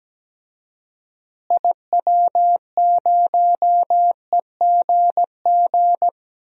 Morse code, heard as a keyed tone, about 17 wpm, 700 Hz. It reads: IW0EGG
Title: DXCC-100 Call Signs - 6 Characters - Repeated 2x - 17wpm